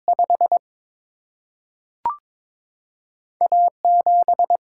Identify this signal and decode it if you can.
Morse code, heard as a keyed tone, about 22 wpm, 700 Hz. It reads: 5 A7
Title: Single-Letter Single-Number 22wpm